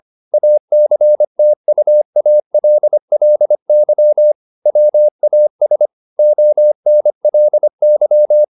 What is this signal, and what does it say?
Morse code, heard as a keyed tone, about 25 wpm, 600 Hz. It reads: ACTUALLY WAS ONLY
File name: the old man and the sea part 1 25wpm